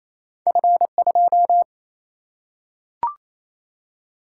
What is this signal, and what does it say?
Morse code, heard as a keyed tone, about 28 wpm, 700 Hz. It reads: F2
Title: Single-Letter Single-Number 28wpm